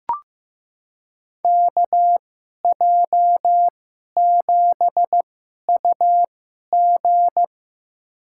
Morse code, heard as a keyed tone, about 15 wpm, spaced as for 11 wpm, 700 Hz. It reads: KJ7UG
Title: USA Call Signs - 2x Character Spacing - 15wpm